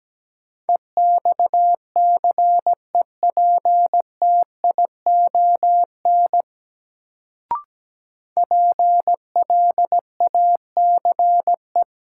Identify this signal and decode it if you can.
Morse code, heard as a keyed tone, about 17 wpm, 700 Hz. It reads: EXCEPTION PLACE
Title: Sets of 4 Words - New Word Drill - 17wpm